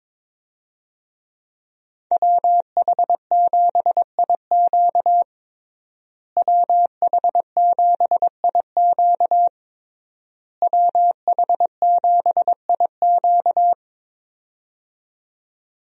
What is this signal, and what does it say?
Morse code, heard as a keyed tone, about 22 wpm, 700 Hz. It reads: WH7IQ WH7IQ WH7IQ
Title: USA Call Signs - 5 Characters - Repeated 3x - 22wpm